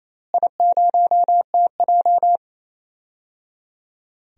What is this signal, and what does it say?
Morse code, heard as a keyed tone, about 28 wpm, 700 Hz. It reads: I0TJ
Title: DXCC-100 Call Signs - 4 Characters - Repeated 3x - 28wpm